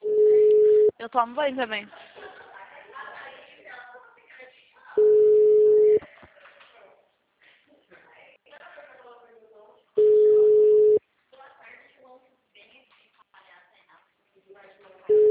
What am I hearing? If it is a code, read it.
Ringback tone (Europe, 425 Hz), with speech.